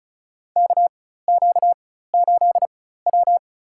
Morse code, heard as a keyed tone, about 36 wpm, spaced as for 16 wpm, 700 Hz. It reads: KQ8W
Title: USA Call Signs - 4 Characters - 4x Character Spacing - 35wpm